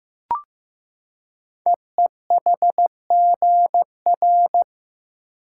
Morse code, heard as a keyed tone, about 15 wpm, 700 Hz. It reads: EEHGR
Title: Code Groups - 5 Letters - 15wpm